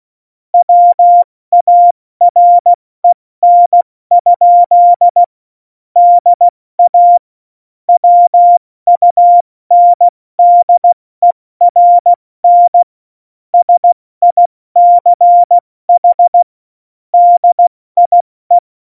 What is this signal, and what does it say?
Morse code, heard as a keyed tone, about 16 wpm, 700 Hz. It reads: WAREN? DA WUNDERN SICH DIE
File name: Morsekurs / Witze / Witz 27. Februar 2018, 70 F80